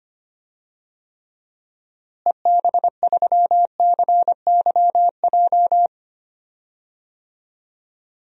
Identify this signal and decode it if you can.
Morse code, heard as a keyed tone, about 25 wpm, 700 Hz. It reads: EB3CYJ
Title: DXCC-100 Call Signs - 6 Characters - Repeated 2x - 25wpm